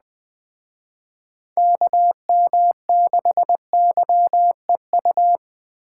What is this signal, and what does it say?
Morse code, heard as a keyed tone, about 21 wpm, 700 Hz. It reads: KM6YEU